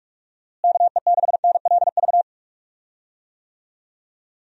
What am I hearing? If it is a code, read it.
Morse code, heard as a keyed tone, about 47 wpm, 700 Hz. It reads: KE6NLV
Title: USA Call Signs - 6 Characters - Repeated 3x - 45wpm